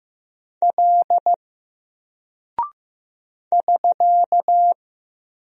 Morse code, heard as a keyed tone, about 15 wpm, 700 Hz. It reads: L <SK>
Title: Single Character (Letter:Number:Punctuation:Prosign) 15wpm